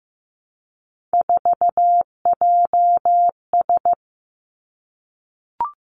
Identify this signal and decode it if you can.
Morse code, heard as a keyed tone, about 15 wpm, 700 Hz. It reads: 4JS
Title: Three Challenging Letter-Number Combinations 15wpm